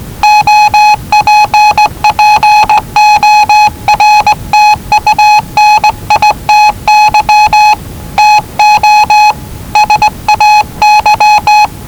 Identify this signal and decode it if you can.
Morse code, heard as a keyed tone, about 19 wpm, 840 Hz. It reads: OPPORTUNITY TO SAY
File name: WellingtonRadioZlwCloseDownOnHfAndMf / 01-Zlw 4277khz 417.5 qrt until 0925z